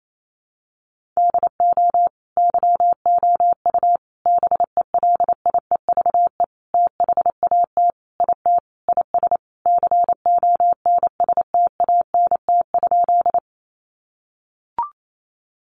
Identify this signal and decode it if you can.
Morse code, heard as a keyed tone, about 28 wpm, 700 Hz. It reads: DO YOU BELIEVE THAT IT IS CONSTANT?
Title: Sentences from Top 1500 Words 28wpm